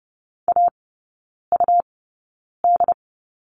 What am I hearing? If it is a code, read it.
Morse code, heard as a keyed tone, about 30 wpm, spaced as for 9 wpm, 690 Hz. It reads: AUD
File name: Lesson 17e - Letter u - Three Characters 30wpm